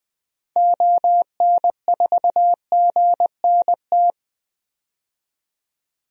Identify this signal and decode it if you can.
Morse code, heard as a keyed tone, about 20 wpm, 700 Hz. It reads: ON4GNT
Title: DXCC-100 Call Signs - 6 Characters - 20wpm